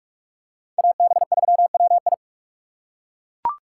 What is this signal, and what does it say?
Morse code, heard as a keyed tone, about 41 wpm, 700 Hz. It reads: AB3WI